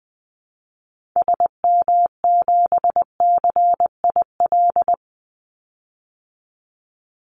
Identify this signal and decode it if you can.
Morse code, heard as a keyed tone, about 20 wpm, 700 Hz. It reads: SM7CIL
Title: Call Signs - 20wpm